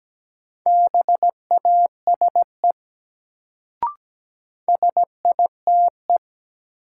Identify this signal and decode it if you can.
Morse code, heard as a keyed tone, about 17 wpm, 700 Hz. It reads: BASE SITE
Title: Sentences from Top 1000 Words - Review Part 1 - New Word Drill - 17wpm